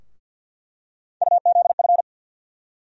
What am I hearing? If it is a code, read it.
Morse code, heard as a keyed tone, about 54 wpm, 700 Hz. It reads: UZF
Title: USA Call Sign Postfixes - 3 Letters 50wpm